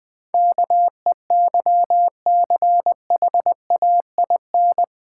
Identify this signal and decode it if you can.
Morse code, heard as a keyed tone, about 20 wpm, 700 Hz. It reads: KEYCHAIN
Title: Easy Words - Speed-Racing with Hints - 20wpm